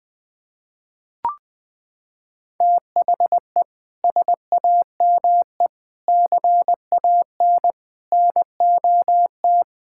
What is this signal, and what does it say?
Morse code, heard as a keyed tone, about 20 wpm, 700 Hz. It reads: THE SAME CANNOT
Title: Sets of 6 Words - 20wpm